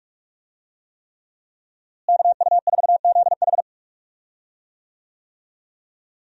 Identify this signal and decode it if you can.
Morse code, heard as a keyed tone, about 46 wpm, 700 Hz. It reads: KU4ZH
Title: USA Call Signs - 5 Characters - Repeated 3x - 45wpm